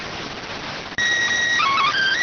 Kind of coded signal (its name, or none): SSTV (slow-scan television)